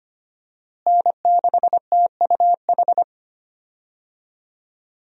Morse code, heard as a keyed tone, about 25 wpm, 700 Hz. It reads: N6TUH
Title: USA Call Signs - 5 Characters - Repeated 3x - 25wpm